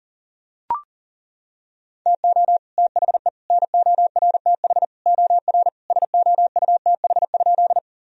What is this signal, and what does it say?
Morse code, heard as a keyed tone, about 40 wpm, 700 Hz. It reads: TO THE NORTH OR SOUTH?
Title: Sentences from Top 1000 Words - Review Part 2 - 40wpm